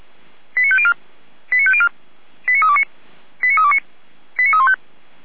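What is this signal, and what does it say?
Five-tone selective call (ZVEI), heard as quick runs of five tones, tones of about 70 ms. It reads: 89504 89504 89329 89329 89315
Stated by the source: ZVEI five-tone selective call, code 89333, received by SDR.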